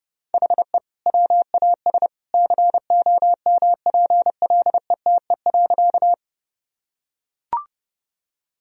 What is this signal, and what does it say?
Morse code, heard as a keyed tone, about 30 wpm, 700 Hz. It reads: HE WAS COMPLETE.
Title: Sentences from Top 1000 Words - Review Part 2 - 30wpm